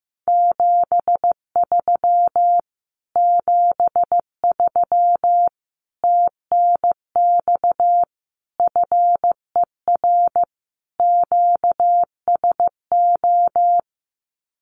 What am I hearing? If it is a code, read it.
Morse code, heard as a keyed tone, about 15 wpm, 700 Hz. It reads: 73 73 TNX FER QSO
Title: Common QSO Phrases Encore 15wpm